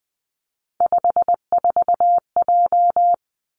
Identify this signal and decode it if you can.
Morse code, heard as a keyed tone, about 20 wpm, 700 Hz. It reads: 54J
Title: Three Challenging Letter-Number Combinations 20wpm